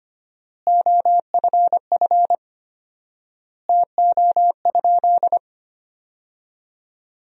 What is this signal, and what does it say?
Morse code, heard as a keyed tone, about 25 wpm, 700 Hz. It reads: OFF TO?
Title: 4x Word-Spacing - Sentences from Top 300 Words - 25wpm